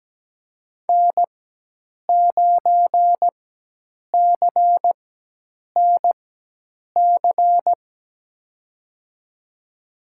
Morse code, heard as a keyed tone, about 17 wpm, spaced as for 8 wpm, 700 Hz. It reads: N9CNC
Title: USA Call Signs - 4x Character Spacing - 17wpm